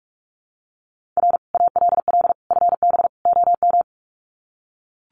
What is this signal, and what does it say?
Morse code, heard as a keyed tone, about 44 wpm, 700 Hz. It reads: R ALL FB OM
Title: Common QSO Phrases 45wpm